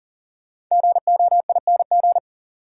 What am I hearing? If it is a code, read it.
Morse code, heard as a keyed tone, about 37 wpm, 700 Hz. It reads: GOING
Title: Sets of 2 Words - New Word Drill - 40wpm